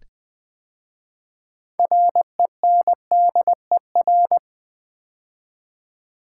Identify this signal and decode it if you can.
Morse code, heard as a keyed tone, about 20 wpm, 700 Hz. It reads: RENDER
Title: 6 Letter Words - First 3 Letters Given - 20wpm